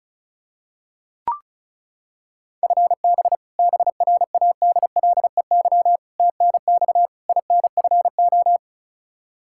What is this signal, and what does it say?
Morse code, heard as a keyed tone, about 36 wpm, 700 Hz. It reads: FB BRADLEY TNX INFO
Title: Practice QSO - Exchange 3 - More Info - 35wpm